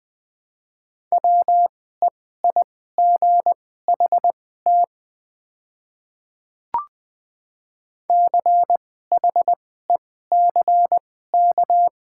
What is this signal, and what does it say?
Morse code, heard as a keyed tone, about 20 wpm, spaced as for 15 wpm, 690 Hz. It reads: WEIGHT CHECK
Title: 2x Character Spacing - Top 500 Words - 20wpm